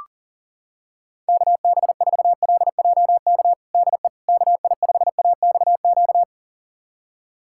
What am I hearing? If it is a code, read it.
Morse code, heard as a keyed tone, about 40 wpm, 700 Hz. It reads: KB4LJK DE KI5AXQ
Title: USA Callsign DE Callsign - 6 Characters - 40wpm